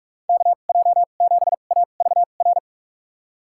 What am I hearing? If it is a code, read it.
Morse code, heard as a keyed tone, about 44 wpm, spaced as for 32 wpm, 700 Hz. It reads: KJ7AVR
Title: USA Call Signs - 2x Character Spacing - 45wpm